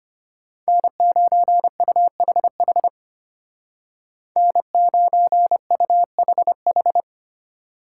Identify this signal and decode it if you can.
Morse code, heard as a keyed tone, about 32 wpm, 700 Hz. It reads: N9UHH N9UHH